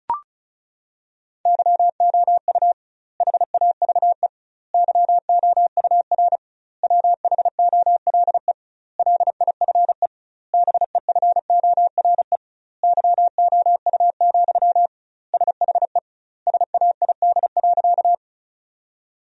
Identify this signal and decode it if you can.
Morse code, heard as a keyed tone, about 36 wpm, 700 Hz. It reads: YOU HAVE YOUR WHOLE LIFE BEFORE YOU, SHE SAID.
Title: 2x Word-Spacing - Sentences from Top 400 Words - 35wpm